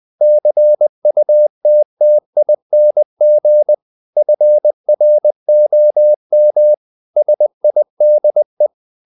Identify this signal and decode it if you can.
Morse code, heard as a keyed tone, about 20 wpm, 600 Hz. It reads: CUTTING FROM SIDE
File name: the old man and the sea part 2 20wpm